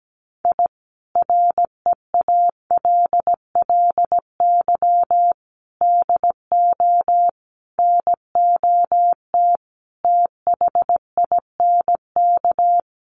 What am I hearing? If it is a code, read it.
Morse code, heard as a keyed tone, about 17 wpm, 700 Hz. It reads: I REALLY DO NOT THINK
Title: Sets of 5 Words Encore2 17wpm